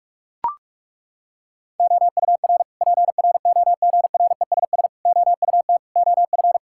Morse code, heard as a keyed tone, about 41 wpm, 700 Hz. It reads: OUR PROGRESS OUT OF
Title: Sentences from Top 1600 Words 45wpm